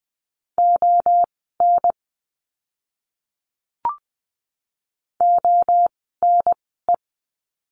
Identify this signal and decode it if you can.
Morse code, heard as a keyed tone, about 20 wpm, spaced as for 15 wpm, 700 Hz. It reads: ON ONE
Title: 061 Lesson 11f - Letter h - Words 20wpm